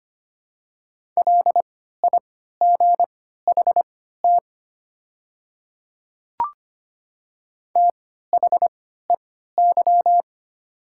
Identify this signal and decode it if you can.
Morse code, heard as a keyed tone, about 25 wpm, spaced as for 14 wpm, 690 Hz. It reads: LIGHT THEY
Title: 3x Character Spacing - Top 200 Words - 25wpm